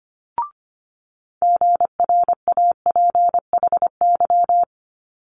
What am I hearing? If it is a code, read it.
Morse code, heard as a keyed tone, about 25 wpm, 700 Hz. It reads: GRAPHY
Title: Multi-letter Phonemes Quick Fire 25wpm